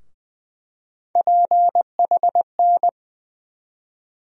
Morse code, heard as a keyed tone, about 21 wpm, 700 Hz. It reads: PHN
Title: USA Call Sign Postfixes - 3 Letters 20wpm